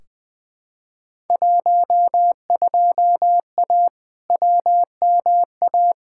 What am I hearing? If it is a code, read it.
Morse code, heard as a keyed tone, about 20 wpm, 700 Hz. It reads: 12A WMA
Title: Field Day Exchanges Encore - 20wpm